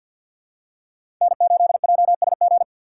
Morse code, heard as a keyed tone, about 51 wpm, 700 Hz. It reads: N8JSG
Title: USA Call Signs - 5 Characters - 6x Character Spacing - 50wpm